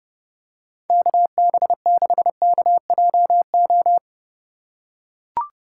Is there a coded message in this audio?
Yes: Morse code from 0.9 to 4.0 s, heard as a keyed tone, about 30 wpm, 700 Hz. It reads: KB6KJO